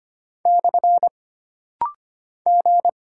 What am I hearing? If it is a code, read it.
Morse code, heard as a keyed tone, about 25 wpm, 700 Hz. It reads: / G
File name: Single Character (Letter:Number:Punctuation:Prosign) - Rapid-Fire - 25wpm